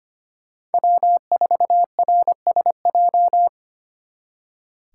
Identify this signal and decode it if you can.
Morse code, heard as a keyed tone, about 33 wpm, 700 Hz. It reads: W4RSJ